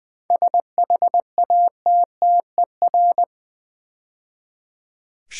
Morse code, heard as a keyed tone, about 20 wpm, 700 Hz. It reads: SHATTER
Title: 7 Letter Words - First 3 Letters Given - 20wpm